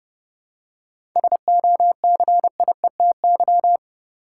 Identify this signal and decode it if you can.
Morse code, heard as a keyed tone, about 32 wpm, 700 Hz. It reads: SOCIETY
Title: Seven Letter Words - 30wpm